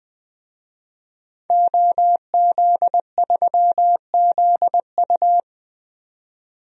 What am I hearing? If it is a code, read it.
Morse code, heard as a keyed tone, about 20 wpm, 700 Hz. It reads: OZ3ZU
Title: DXCC-100 Call Signs - 5 Characters - 20wpm